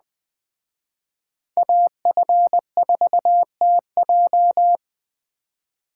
Morse code, heard as a keyed tone, about 20 wpm, 700 Hz. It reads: AF4TJ
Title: Speed Racing - USA 2x2 Call Signs 17wpm